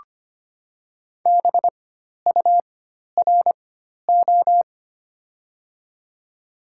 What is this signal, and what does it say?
Morse code, heard as a keyed tone, about 25 wpm, spaced as for 12 wpm, 700 Hz. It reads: BURO
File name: Top 90 QSO Elements - 4x Character Spacing - 25wpm